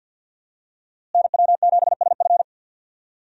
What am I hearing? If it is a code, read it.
Morse code, heard as a keyed tone, about 52 wpm, 700 Hz. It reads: NW7SF
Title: USA Call Signs - 5 Characters - Repeated 3x - 50wpm